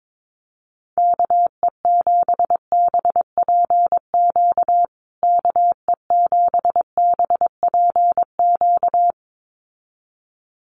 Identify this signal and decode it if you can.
Morse code, heard as a keyed tone, about 22 wpm, 700 Hz. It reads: KE7BPQ KE7BPQ
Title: USA Call Signs Repeated 2x 22wpm